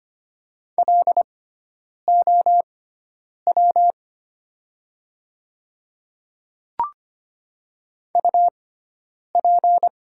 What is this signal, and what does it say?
Morse code, heard as a keyed tone, about 25 wpm, spaced as for 9 wpm, 700 Hz. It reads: LOW UP
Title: 6x Character Spacing - Top 200 Words - 25wpm